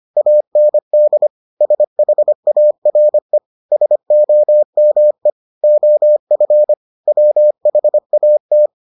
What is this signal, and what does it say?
Morse code, heard as a keyed tone, about 25 wpm, 600 Hz. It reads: AND SHARE SOME OF WHAT